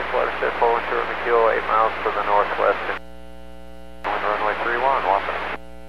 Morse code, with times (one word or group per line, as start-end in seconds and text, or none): none